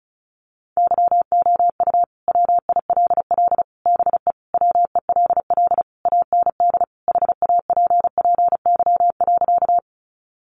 Morse code, heard as a keyed tone, about 34 wpm, 700 Hz. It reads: YOU WILL BE WELL AND HAPPY.
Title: Sentences from Top 1000 Words 35wpm